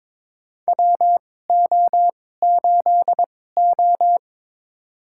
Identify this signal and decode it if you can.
Morse code, heard as a keyed tone, about 21 wpm, spaced as for 16 wpm, 700 Hz. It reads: WO8O
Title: USA Call Signs - 4 Characters - 2x Character Spacing - 22wpm